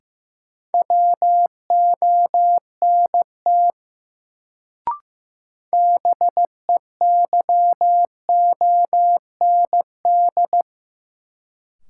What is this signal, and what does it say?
Morse code, heard as a keyed tone, about 15 wpm, 700 Hz. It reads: WONT BEYOND
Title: Sentences from Top 1000 Words - Review Part 6 - New Word Drill - 15wpm